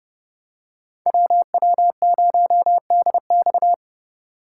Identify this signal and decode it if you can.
Morse code, heard as a keyed tone, about 28 wpm, 700 Hz. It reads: WW0DX